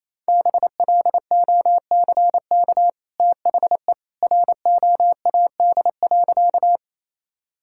Morse code, heard as a keyed tone, about 28 wpm, 700 Hz. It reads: BLOCK THE ROAD.
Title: Sentences from Top 1600 Words 28wpm